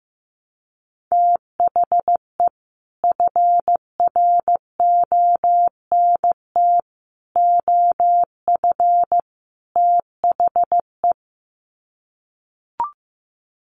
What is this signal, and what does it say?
Morse code, heard as a keyed tone, about 15 wpm, 700 Hz. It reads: THE FRONT OF THE
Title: Sets of 4 Words 15wpm